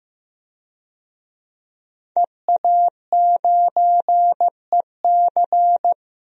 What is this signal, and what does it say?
Morse code, heard as a keyed tone, about 15 wpm, 700 Hz. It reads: EA9EC
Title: DXCC-100 Call Signs - 5 Characters - 15wpm